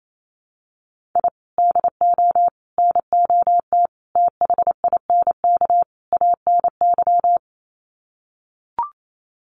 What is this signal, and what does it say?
Morse code, heard as a keyed tone, about 28 wpm, 700 Hz. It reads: I DO NOT THINK ANY
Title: Sets of 5 Words Encore 28wpm